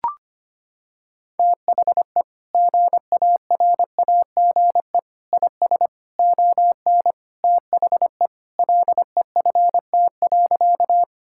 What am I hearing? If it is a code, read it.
Morse code, heard as a keyed tone, about 25 wpm, 700 Hz. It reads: THE GARAGE IS ON THE LEFT.
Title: Sentences from Top 1600 Words 25wpm